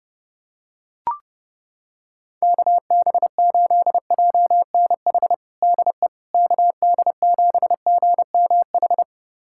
Morse code, heard as a keyed tone, about 30 wpm, 700 Hz. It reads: KB8JNH DE KD7GMH
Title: USA Callsign DE Callsign - 6 Characters - 30wpm